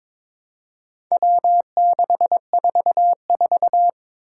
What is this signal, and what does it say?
Morse code, heard as a keyed tone, about 22 wpm, 700 Hz. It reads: W644
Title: Four Challenging Letter-Number Combinations 22wpm